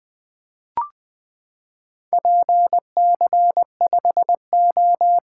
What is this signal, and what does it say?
Morse code, heard as a keyed tone, about 20 wpm, 700 Hz. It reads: PC5O
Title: Four Letter-Number Combinations - 20wpm